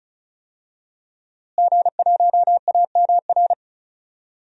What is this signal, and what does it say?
Morse code, heard as a keyed tone, about 36 wpm, 700 Hz. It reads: G1AMR
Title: DXCC-100 Call Signs - 5 Characters - Repeated 3x - 35wpm